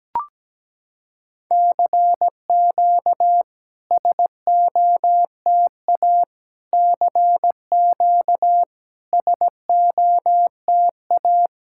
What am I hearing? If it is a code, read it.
Morse code, heard as a keyed tone, about 17 wpm, 700 Hz. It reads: CQ SOTA CQ SOTA
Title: SOTA QSO Practice 17wpm